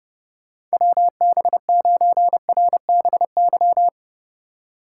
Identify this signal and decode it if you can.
Morse code, heard as a keyed tone, about 32 wpm, 700 Hz. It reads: WB9RBY
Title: USA Call Signs - 6 Characters - 4x Character Spacing - 30wpm